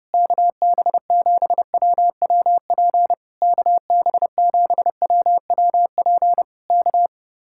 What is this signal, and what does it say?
Morse code, heard as a keyed tone, about 30 wpm, 700 Hz. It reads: KB7WWP KB7WWP K